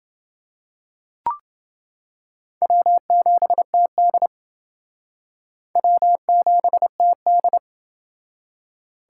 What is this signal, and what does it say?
Morse code, heard as a keyed tone, about 30 wpm, 700 Hz. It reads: W7TD W7TD